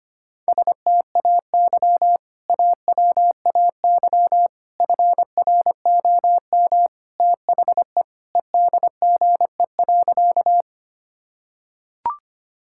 Morse code, heard as a keyed tone, about 25 wpm, 700 Hz. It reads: STAY AWAY FROM THE EDGE.
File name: practice / 25wpm / Sentences with 3-Word Phrases - 25wpm